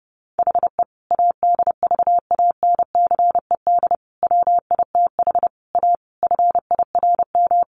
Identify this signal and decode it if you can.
Morse code, heard as a keyed tone, about 28 wpm, 700 Hz. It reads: HE ADVANCED WITH A FIRM
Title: Sentences from Top 1400 Words 30wpm